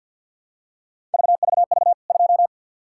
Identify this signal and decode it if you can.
Morse code, heard as a keyed tone, about 48 wpm, 700 Hz. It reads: VVV 2
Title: K1USN SST QSO Practice 50wpm